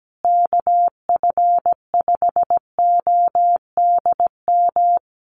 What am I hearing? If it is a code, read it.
Morse code, heard as a keyed tone, about 17 wpm, 700 Hz. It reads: KF5ODM
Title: USA Call Signs Repeated 3x 17wpm